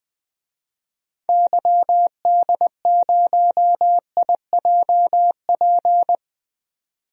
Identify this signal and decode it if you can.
Morse code, heard as a keyed tone, about 20 wpm, 700 Hz. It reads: YD0IJP
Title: DXCC-100 Call Signs - 6 Characters - 20wpm